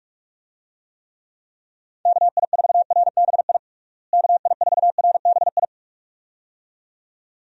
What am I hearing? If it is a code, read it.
Morse code, heard as a keyed tone, about 41 wpm, 700 Hz. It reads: KI4RBI KI4RBI
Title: USA Call Signs - 6 Characters - Repeated 2x - 45wpm